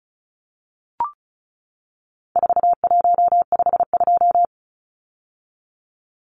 Morse code, heard as a keyed tone, about 34 wpm, 700 Hz. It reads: 4152
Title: Numbers - 4-Digit 35wpm